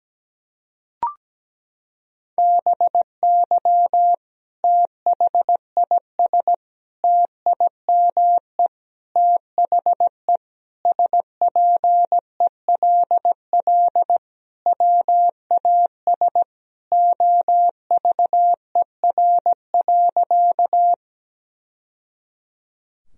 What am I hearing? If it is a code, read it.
Morse code, heard as a keyed tone, about 17 wpm, 700 Hz. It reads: BY THIS TIME THE SPELL WAS OVER.
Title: Sentences from Top 1000 Words - Review Part 1 - 17wpm